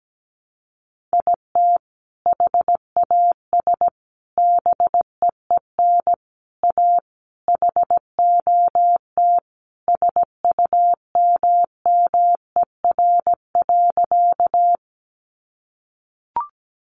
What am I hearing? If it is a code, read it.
Morse code, heard as a keyed tone, about 17 wpm, 700 Hz. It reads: IT HAS BEEN A HOT SUMMER.
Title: Sentences from Top 1100 Words 17wpm